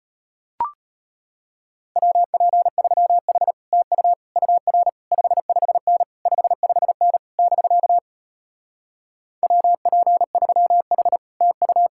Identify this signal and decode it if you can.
Morse code, heard as a keyed tone, about 33 wpm, 700 Hz. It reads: WP3H TU UR 55N 55N <BK> WP3H TU